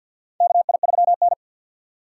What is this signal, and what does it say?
Morse code, heard as a keyed tone, about 53 wpm, 700 Hz. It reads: KI3N
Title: USA Call Signs - 4 Characters - 6x Character Spacing - 50wpm